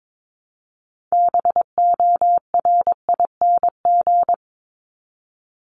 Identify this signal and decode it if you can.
Morse code, heard as a keyed tone, about 22 wpm, 700 Hz. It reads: BORING
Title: Top 1501-1750 Words 22wpm